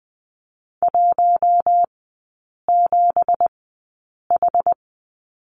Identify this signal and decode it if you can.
Morse code, heard as a keyed tone, about 20 wpm, spaced as for 8 wpm, 690 Hz. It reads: 17H
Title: 205 Lesson 33e - Letter j - Three Characters 20wpm